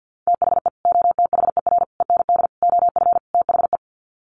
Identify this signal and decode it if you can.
Morse code, heard as a keyed tone, about 50 wpm, 700 Hz. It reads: THE OTHER END OF THE
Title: Sets of 5 Words 50wpm